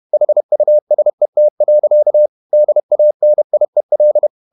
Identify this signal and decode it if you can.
Morse code, heard as a keyed tone, about 31 wpm, 600 Hz. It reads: HUSET. DANIEL
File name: Ferie QRQ 31wpm